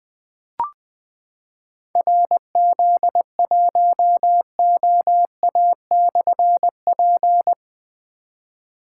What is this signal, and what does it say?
Morse code, heard as a keyed tone, about 20 wpm, 700 Hz. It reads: RZ1OA/P
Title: Difficult Call Signs 20wpm